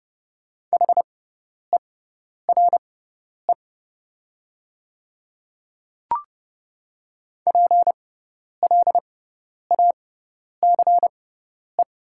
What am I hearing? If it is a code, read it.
Morse code, heard as a keyed tone, about 30 wpm, spaced as for 10 wpm, 700 Hz. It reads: HERE PLACE
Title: 6x Character Spacing - Top 300 Words - 30wpm